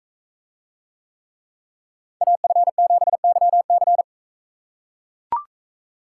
Morse code, heard as a keyed tone, about 43 wpm, 700 Hz. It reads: AF7YC